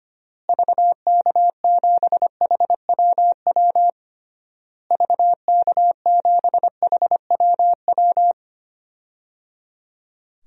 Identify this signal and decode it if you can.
Morse code, heard as a keyed tone, about 25 wpm, 700 Hz. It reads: VK7HWW VK7HWW
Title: DXCC-100 Call Signs - 6 Characters - Repeated 3x - 25wpm